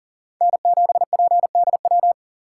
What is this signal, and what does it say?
Morse code, heard as a keyed tone, about 40 wpm, 700 Hz. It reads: N7PDW
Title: USA Call Signs - 5 Characters - 4x Character Spacing - 40wpm